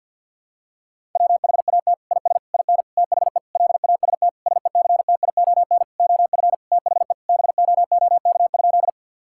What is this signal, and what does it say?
Morse code, heard as a keyed tone, about 51 wpm, 700 Hz. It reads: WHAT IS IN THE LAST SECTION OF THE BOOK?
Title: Sentences from Top 1000 Words - Review Part 5 - 50wpm